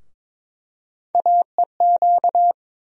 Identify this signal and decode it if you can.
Morse code, heard as a keyed tone, about 22 wpm, 700 Hz. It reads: AEQ
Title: USA Call Sign Postfixes - 3 Letters 22wpm